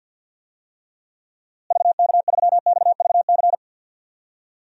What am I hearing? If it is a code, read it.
Morse code, heard as a keyed tone, about 52 wpm, 700 Hz. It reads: VK3XVC
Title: DXCC-100 Call Signs - 6 Characters - Repeated 3x - 50wpm